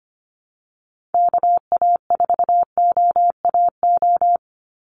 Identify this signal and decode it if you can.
Morse code, heard as a keyed tone, about 25 wpm, 700 Hz. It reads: KA4OAO
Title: USA Call Signs 25wpm